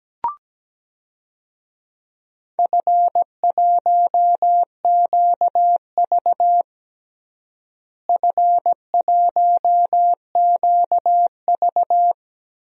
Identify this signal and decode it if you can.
Morse code, heard as a keyed tone, about 17 wpm, 700 Hz. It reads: F1QV F1QV
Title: DXCC-100 Call Signs - 4 Characters - Repeated 2x - 17wpm